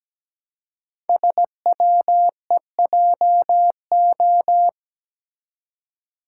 Morse code, heard as a keyed tone, about 17 wpm, 700 Hz. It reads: SWEJO
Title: Code Groups - 5 Letters - 17wpm